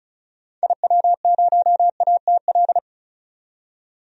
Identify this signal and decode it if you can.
Morse code, heard as a keyed tone, about 36 wpm, 700 Hz. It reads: IW0ATL
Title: DXCC-100 Call Signs - 6 Characters - 35wpm